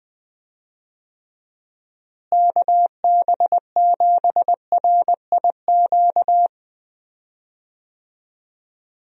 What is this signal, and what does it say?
Morse code, heard as a keyed tone, about 20 wpm, 700 Hz. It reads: KB7RIQ
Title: USA Call Signs - 6 Characters - Repeated 3x - 20wpm